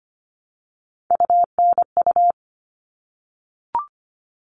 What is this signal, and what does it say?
Morse code, heard as a keyed tone, about 25 wpm, 700 Hz. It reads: UNU
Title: All Cut Numbers - 3-Digit 25wpm